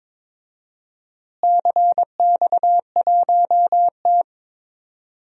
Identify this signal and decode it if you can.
Morse code, heard as a keyed tone, about 22 wpm, 700 Hz. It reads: CX1T
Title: DXCC-100 Call Signs - 4 Characters - Repeated 2x - 22wpm